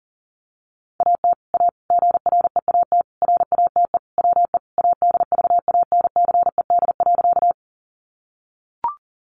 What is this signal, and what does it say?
Morse code, heard as a keyed tone, about 40 wpm, 700 Hz. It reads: AT A GREAT RATE WE ADVANCED.
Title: Sentences from Top 1200 Words 40wpm